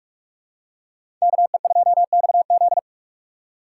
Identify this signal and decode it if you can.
Morse code, heard as a keyed tone, about 41 wpm, 700 Hz. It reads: KE2XZ
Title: USA Call Signs - 5 Characters - 2x Character Spacing - 45wpm